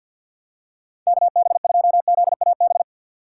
Morse code, heard as a keyed tone, about 53 wpm, 700 Hz. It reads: KB2ZAB
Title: USA Call Signs - 6 Characters - 6x Character Spacing - 50wpm